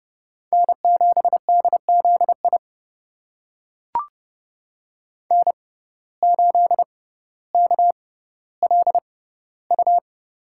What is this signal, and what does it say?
Morse code, heard as a keyed tone, about 30 wpm, spaced as for 15 wpm, 700 Hz. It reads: N7DZI N8KLU
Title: USA Call Signs - 5 Characters - 6x Character Spacing - 30wpm